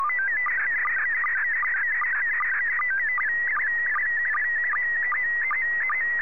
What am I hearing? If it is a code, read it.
SSTV (slow-scan television), PD 50.
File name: sstv / PD50-VK3DMX